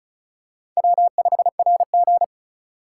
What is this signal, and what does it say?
Morse code, heard as a keyed tone, about 34 wpm, 700 Hz. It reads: W5RG
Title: USA Call Signs - 4 Characters - 2x Character Spacing - 35wpm